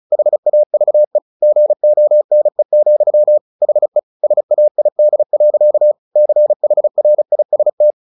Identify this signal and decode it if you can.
Morse code, heard as a keyed tone, about 34 wpm, 600 Hz. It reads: HAVE GONE, HE SAID. CHRIST
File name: the old man and the sea / the old man and the sea 1 35wpm